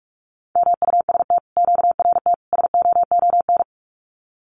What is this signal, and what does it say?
Morse code, heard as a keyed tone, about 46 wpm, 700 Hz. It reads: MUST QRT SOON
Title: Common QSO Phrases Encore 45wpm